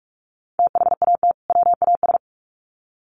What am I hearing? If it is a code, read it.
Morse code, heard as a keyed tone, about 44 wpm, 700 Hz. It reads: THAT WAS
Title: Sets of 2 Words 45wpm